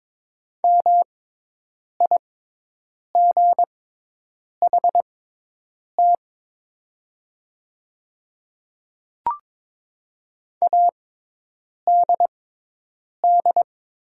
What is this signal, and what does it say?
Morse code, heard as a keyed tone, about 22 wpm, spaced as for 8 wpm, 700 Hz. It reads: MIGHT ADD
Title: 6x Character Spacing - Top 300 Words - 22wpm